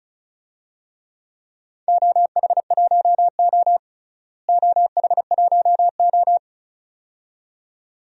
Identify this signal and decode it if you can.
Morse code, heard as a keyed tone, about 34 wpm, 700 Hz. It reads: OH1O OH1O
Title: DXCC-100 Call Signs - 4 Characters - Repeated 2x - 35wpm